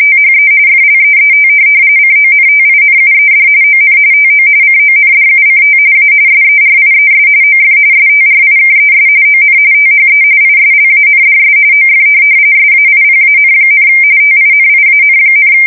RTTY, about 45.45 baud. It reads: THE FLEET CONTINUED ON TO CONDUCT A SERIES OF STRIKES ON THE ISLANDS OF SAIPAN, TINIAN,
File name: fsk170 / USS-Massachusetts